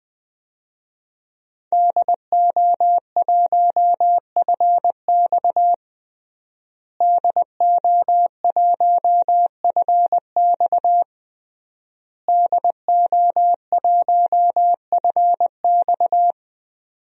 Morse code, heard as a keyed tone, about 19 wpm, 700 Hz. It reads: DO1FX DO1FX DO1FX
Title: DXCC-100 Call Signs - 5 Characters - Repeated 3x - 20wpm